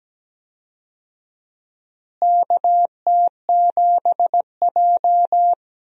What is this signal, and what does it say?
Morse code, heard as a keyed tone, about 17 wpm, 700 Hz. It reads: KT7J
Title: USA Call Signs - 4 Characters - Repeated 2x - 17wpm